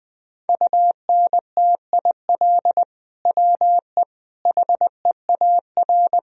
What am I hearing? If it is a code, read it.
Morse code, heard as a keyed tone, about 20 wpm, 700 Hz. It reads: UNTIL WE HEAR
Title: Sentences from Top 1000 Words - Review Part 1 - 20wpm